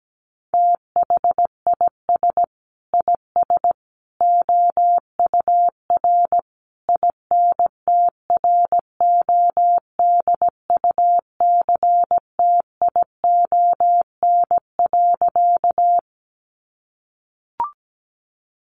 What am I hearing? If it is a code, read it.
Morse code, heard as a keyed tone, about 17 wpm, 700 Hz. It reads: THIS IS OUR INTRODUCTION.
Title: Sentences from Top 1000 Words 17wpm